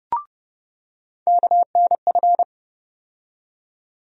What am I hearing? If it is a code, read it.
Morse code, heard as a keyed tone, about 30 wpm, 700 Hz. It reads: KNF
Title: USA Call Sign Postfixes - 3 Letters 30wpm